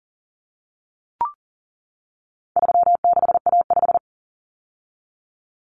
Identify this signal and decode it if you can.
Morse code, heard as a keyed tone, about 38 wpm, 700 Hz. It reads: 36A5
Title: Common Cut Numbers - 4-Digit 40wpm